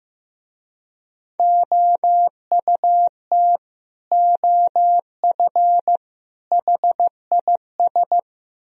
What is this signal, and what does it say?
Morse code, heard as a keyed tone, about 15 wpm, 700 Hz. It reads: OUT OF HIS
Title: Speed-Racing - Sets of 3 Words 15wpm